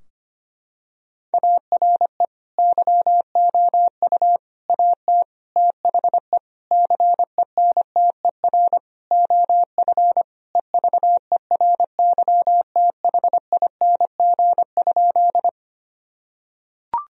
Morse code, heard as a keyed tone, about 25 wpm, 700 Hz. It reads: ARE YOU AT THE CENTER OF EVERYTHING?
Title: Sentences from Top 1000 Words - Review Part 2 - 25wpm